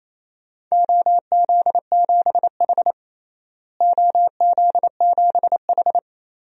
Morse code, heard as a keyed tone, about 28 wpm, 700 Hz. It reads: OZ7H OZ7H
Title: DXCC-100 Call Signs - 4 Characters - Repeated 2x - 28wpm